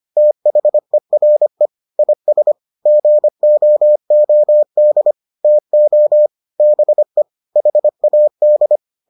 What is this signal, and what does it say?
Morse code, heard as a keyed tone, about 25 wpm, 600 Hz. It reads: THERE IS GOOD TO BE HAD